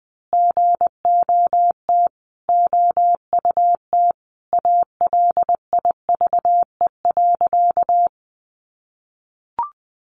Morse code, heard as a keyed tone, about 20 wpm, 700 Hz. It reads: GOT OUT ALIVE.
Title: A Sound Of Thunder 20wpm